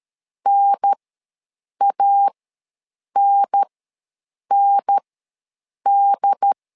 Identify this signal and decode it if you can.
Morse code, heard as a keyed tone, about 13 wpm, spaced as for 7 wpm, 800 Hz. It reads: NANND
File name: LEZIONI / 2 UHDN / 2 UHDN ws 12wpm 1708